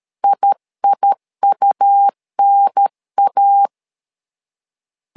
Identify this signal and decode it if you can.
Morse code, heard as a keyed tone, about 13 wpm, 800 Hz. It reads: IIUNA
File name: LEZIONI / 2 UHDN / 2 UHDN 12wpm 1456